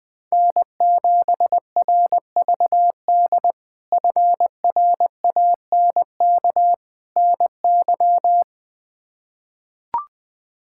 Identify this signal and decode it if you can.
Morse code, heard as a keyed tone, about 20 wpm, 700 Hz. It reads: N7RVD FRANK NY
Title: K1USN SST QSO Practice - 20wpm